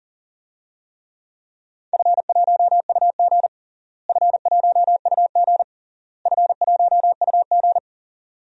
Morse code, heard as a keyed tone, about 40 wpm, 700 Hz. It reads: F1UG F1UG F1UG
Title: DXCC-100 Call Signs - 4 Characters - Repeated 3x - 40wpm